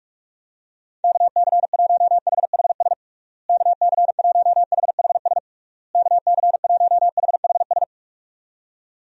Morse code, heard as a keyed tone, about 41 wpm, 700 Hz. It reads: KC1HHS KC1HHS KC1HHS
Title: USA Call Signs - 6 Characters - Repeated 3x - 45wpm